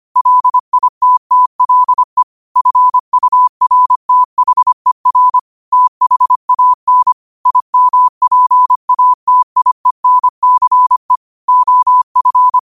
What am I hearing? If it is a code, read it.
Morse code, heard as a keyed tone, about 25 wpm, 1000 Hz. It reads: LITTLE FURTHER THAN IMPATIENCE OF